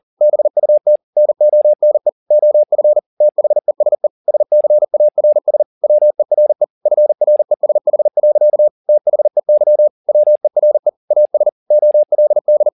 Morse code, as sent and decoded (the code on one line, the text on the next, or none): -... ..- - / -. --- -. / --- ..-. / - .... ... / ... -.-. .- .-. ... / .-- . .-. / ..-. .-. ... .... .-.-.- / - .... -.-- / .-- . .-. / .- ... / --- .-.. -..
BUT NONE OF THESE SCARS WERE FRESH. THEY WERE AS OLD